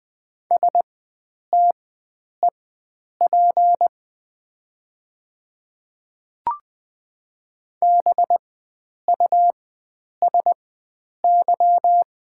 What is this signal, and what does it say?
Morse code, heard as a keyed tone, about 20 wpm, spaced as for 9 wpm, 690 Hz. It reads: STEP BUSY
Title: 4x Character Spacing - Top 500 Words - 20wpm